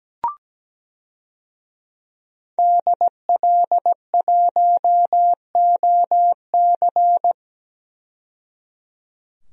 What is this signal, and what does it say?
Morse code, heard as a keyed tone, about 17 wpm, 700 Hz. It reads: DL1OC
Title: DXCC-100 Call Signs - 5 Characters - 17wpm